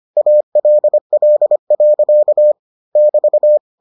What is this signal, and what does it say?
Morse code, heard as a keyed tone, about 25 wpm, 600 Hz. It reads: ALL. <BT>